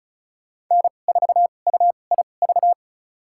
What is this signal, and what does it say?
Morse code, heard as a keyed tone, about 36 wpm, spaced as for 25 wpm, 700 Hz. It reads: N4UIV
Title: USA Call Signs - 5 Characters - 2x Character Spacing - 35wpm